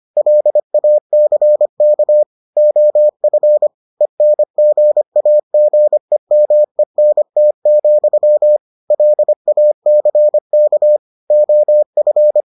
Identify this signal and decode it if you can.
Morse code, heard as a keyed tone, about 25 wpm, 600 Hz. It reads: LACK OF ENGAGEMENT, LACK OF